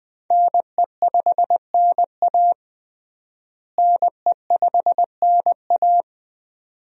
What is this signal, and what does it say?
Morse code, heard as a keyed tone, about 20 wpm, 700 Hz. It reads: NE5NA NE5NA
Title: USA Call Signs - 5 Characters - Repeated 2x - 20wpm